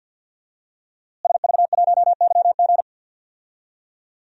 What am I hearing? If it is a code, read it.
Morse code, heard as a keyed tone, about 50 wpm, 700 Hz. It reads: SV1YG
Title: DXCC-100 Call Signs - 5 Characters - Repeated 2x - 50wpm